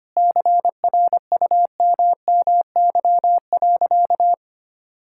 Morse code, heard as a keyed tone, about 25 wpm, 700 Hz. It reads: CRUMMY.
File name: practice / 25wpm / All-Age Appropriate Jokes - 25wpm